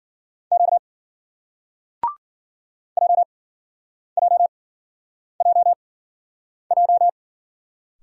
Morse code, heard as a keyed tone, about 51 wpm, spaced as for 9 wpm, 700 Hz. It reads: Q JJJJ